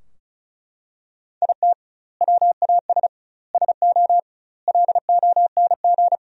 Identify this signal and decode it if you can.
Morse code, heard as a keyed tone, about 34 wpm, 700 Hz. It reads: IT WAS SO LONG
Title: 2x Word-Spacing - Sentences from Top 500 Words - 35wpm